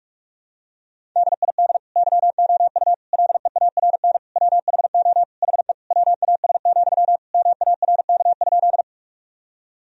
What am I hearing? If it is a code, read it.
Morse code, heard as a keyed tone, about 46 wpm, 700 Hz. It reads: DID YOU LEARN WHO HE WAS, MARK?
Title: Sentences from Top 1000 Words - Review Part 1 - 45wpm